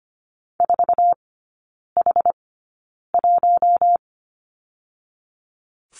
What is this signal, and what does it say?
Morse code, heard as a keyed tone, about 25 wpm, spaced as for 9 wpm, 700 Hz. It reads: 4H1
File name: Lesson 15e - Number 5 - Three Characters 25wpm